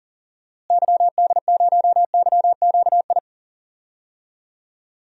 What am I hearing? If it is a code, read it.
Morse code, heard as a keyed tone, about 40 wpm, 700 Hz. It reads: YD0YQI
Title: DXCC-100 Call Signs - 6 Characters - Repeated 2x - 40wpm